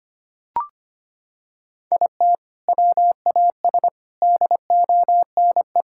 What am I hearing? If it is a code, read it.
Morse code, heard as a keyed tone, about 25 wpm, 700 Hz. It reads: IT WAS DONE
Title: Sentences from Top 1000 Words - Review Part 5 - 25wpm